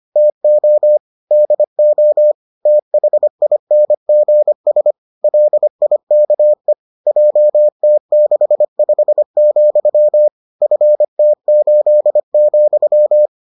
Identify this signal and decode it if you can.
Morse code, heard as a keyed tone, about 25 wpm, 600 Hz. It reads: TO DO THINGS LIKE JT65, FT8,